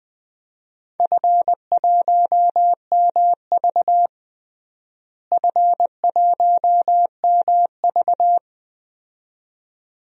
Morse code, heard as a keyed tone, about 20 wpm, 700 Hz. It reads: F1MV F1MV
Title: DXCC-100 Call Signs - 4 Characters - Repeated 2x - 20wpm